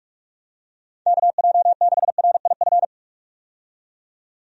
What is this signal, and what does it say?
Morse code, heard as a keyed tone, about 46 wpm, 700 Hz. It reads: KJ6RIF
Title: USA Call Signs - 6 Characters - 45wpm